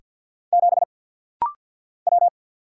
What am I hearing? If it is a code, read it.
Morse code, heard as a keyed tone, about 49 wpm, 700 Hz. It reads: 7 W
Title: Single Letter-Number - Rapid-Fire 50wpm